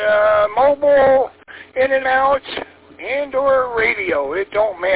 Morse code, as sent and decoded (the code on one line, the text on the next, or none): none